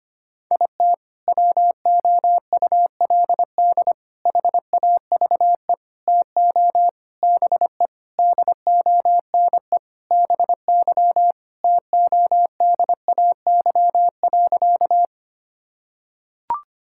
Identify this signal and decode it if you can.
Morse code, heard as a keyed tone, about 25 wpm, 700 Hz. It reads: IT WOULD HAVE TO BE DONE BY TODAY.
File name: practice / 25wpm / Sentences with 4-Word Phrases - 25wpm